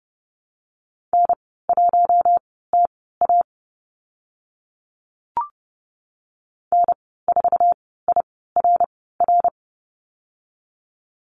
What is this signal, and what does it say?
Morse code, heard as a keyed tone, about 30 wpm, spaced as for 17 wpm, 700 Hz. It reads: N1TA N4IRR
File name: Lesson 15g - Number 5 - Call Signs 30wpm